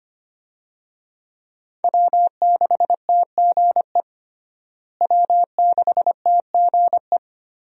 Morse code, heard as a keyed tone, about 25 wpm, 700 Hz. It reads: W6TGE W6TGE
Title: USA Call Signs - 5 Characters - Repeated 3x - 25wpm